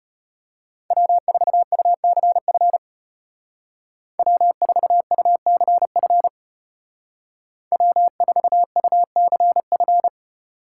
Morse code, heard as a keyed tone, about 33 wpm, 700 Hz. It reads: W4UCF W4UCF W4UCF